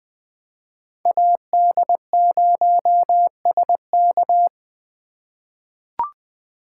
Morse code, heard as a keyed tone, about 21 wpm, 700 Hz. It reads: AD0SK
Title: USA Call Signs - 2x Character Spacing - 20wpm